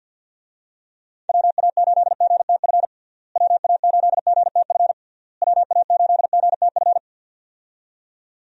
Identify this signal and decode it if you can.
Morse code, heard as a keyed tone, about 52 wpm, 700 Hz. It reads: WA8GTF WA8GTF WA8GTF
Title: USA Call Signs - 6 Characters - Repeated 3x - 50wpm